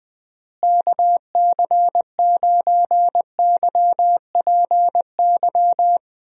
Morse code, heard as a keyed tone, about 20 wpm, 700 Hz. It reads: KC9YPY
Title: USA Call Signs - 6 Characters - Repeated 2x - 20wpm